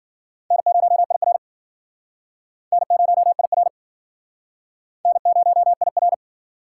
Morse code, heard as a keyed tone, about 52 wpm, 700 Hz. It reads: N0IR N0IR N0IR